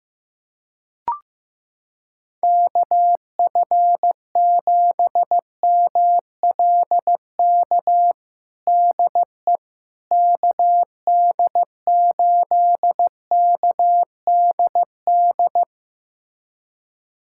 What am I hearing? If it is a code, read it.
Morse code, heard as a keyed tone, about 15 wpm, 700 Hz. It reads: KF7MLK DE KD8KDD
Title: USA Callsign DE Callsign 15wpm